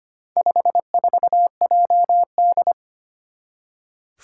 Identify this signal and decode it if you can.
Morse code, heard as a keyed tone, about 25 wpm, 700 Hz. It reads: 54JD
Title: Four Challenging Letter-Number Combinations 25wpm